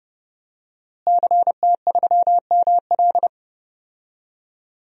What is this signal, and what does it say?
Morse code, heard as a keyed tone, about 32 wpm, 700 Hz. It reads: CT3ML
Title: DXCC-100 Call Signs - 5 Characters - Repeated 3x - 30wpm